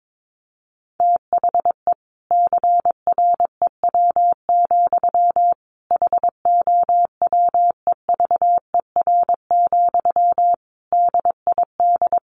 Morse code, heard as a keyed tone, about 22 wpm, 700 Hz. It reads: THE CREW, HOWEVER, DID
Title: Sentences from Top 1500 Words 22wpm